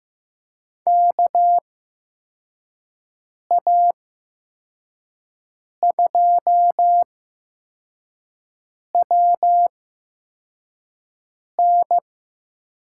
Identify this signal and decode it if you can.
Morse code, heard as a keyed tone, about 15 wpm, spaced as for 4 wpm, 700 Hz. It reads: KA2WN
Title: USA Call Signs - 8x Character Spacing - 15wpm